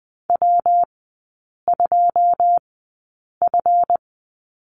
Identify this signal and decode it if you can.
Morse code, heard as a keyed tone, about 20 wpm, spaced as for 8 wpm, 700 Hz. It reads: W2F
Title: 166 Lesson 27e - Number 7 - Three Characters 20wpm